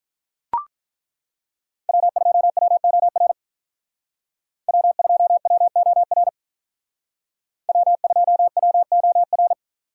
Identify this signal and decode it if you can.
Morse code, heard as a keyed tone, about 42 wpm, 700 Hz. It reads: W2WOR W2WOR W2WOR
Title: Speed Racing - USA 1x3 Call Signs 35wpm